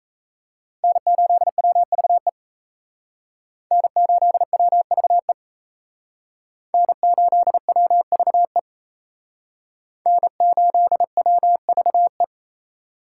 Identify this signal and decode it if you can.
Morse code, heard as a keyed tone, about 35 wpm, 700 Hz. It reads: N8WVE N8WVE N8WVE N8WVE
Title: Speed Racing - USA 1x3 Call Signs 28wpm